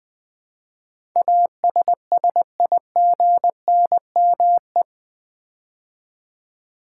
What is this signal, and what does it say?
Morse code, heard as a keyed tone, about 20 wpm, 700 Hz. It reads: ASSIGNME
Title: Build A Word - 4 Letters At A Time - 10-Letter Words - 20wpm